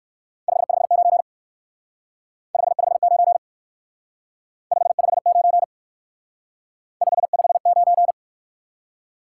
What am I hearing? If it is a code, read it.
Morse code, heard as a keyed tone, about 63 wpm, 700 Hz. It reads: 559 559 559 559